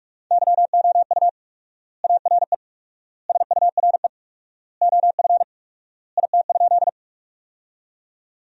Morse code, heard as a keyed tone, about 46 wpm, 700 Hz. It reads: YOU ARE SURE OF IT?
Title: 4x Word-Spacing - Sentences from Top 300 Words - 45wpm